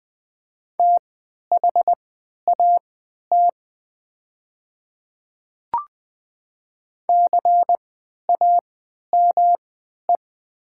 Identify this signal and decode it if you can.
Morse code, heard as a keyed tone, about 20 wpm, spaced as for 11 wpm, 690 Hz. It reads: THAT CAME
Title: 3x Character Spacing - Top 200 Words - 20wpm